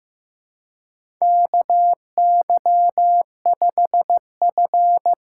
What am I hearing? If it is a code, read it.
Morse code, heard as a keyed tone, about 15 wpm, 700 Hz. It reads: KY5F
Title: USA Call Signs - 4 Characters - 4x Character Spacing - 15wpm